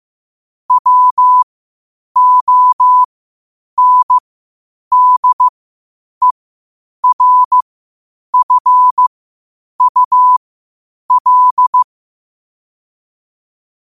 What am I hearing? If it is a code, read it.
Morse code, heard as a keyed tone, about 15 wpm, spaced as for 8 wpm, 1000 Hz. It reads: WONDERFUL
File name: qotdfiles / 8wpm